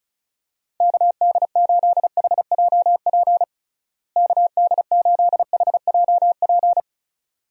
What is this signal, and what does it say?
Morse code, heard as a keyed tone, about 36 wpm, 700 Hz. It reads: KD8HJP KD8HJP
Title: USA Call Signs - 6 Characters - Repeated 2x - 35wpm